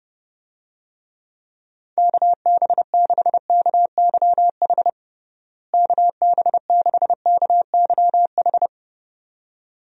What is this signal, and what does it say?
Morse code, heard as a keyed tone, about 30 wpm, 700 Hz. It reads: KB6KYH KB6KYH
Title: USA Call Signs - 6 Characters - Repeated 2x - 30wpm